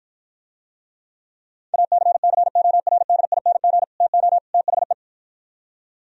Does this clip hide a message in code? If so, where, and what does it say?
Yes: Morse code from 1.7 to 4.9 s, heard as a keyed tone, about 52 wpm, 700 Hz. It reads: ACCORDING TO THE